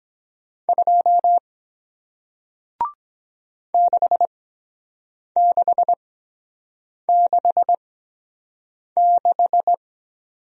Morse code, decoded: 2 6666

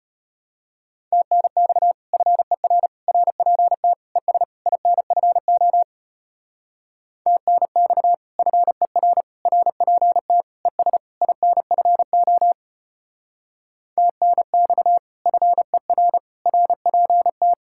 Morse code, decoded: TNX FER RPT ES INFO TNX FER RPT ES INFO TNX FER RPT

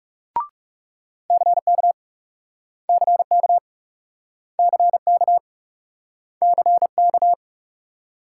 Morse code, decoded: CK CK CK CK

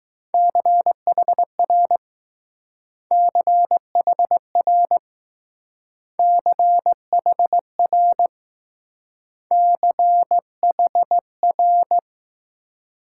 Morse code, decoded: CHR CHR CHR CHR

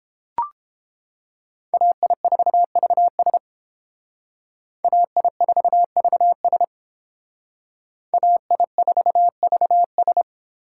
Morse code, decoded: AI4VS AI4VS AI4VS

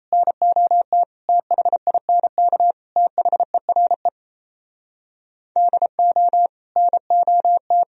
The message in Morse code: NOT THINK THERE DO NOT

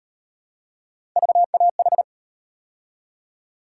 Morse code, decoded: VAH

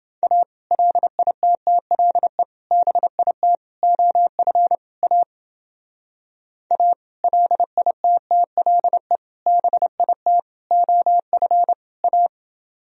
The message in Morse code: A LITTLE BIT OF A A LITTLE BIT OF A

